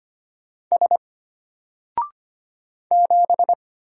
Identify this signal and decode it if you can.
Morse code, heard as a keyed tone, about 25 wpm, 700 Hz. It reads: S 7
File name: Single-Letter-Number-25wpm